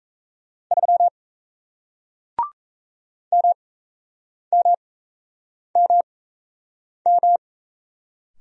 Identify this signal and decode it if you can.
Morse code, heard as a keyed tone, about 37 wpm, spaced as for 8 wpm, 700 Hz. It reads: 3 MMMM